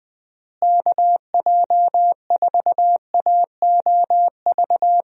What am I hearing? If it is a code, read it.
Morse code, heard as a keyed tone, about 20 wpm, 700 Hz. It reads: KJ4AOV
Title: USA Call Signs - 6 Characters - Repeated 2x - 20wpm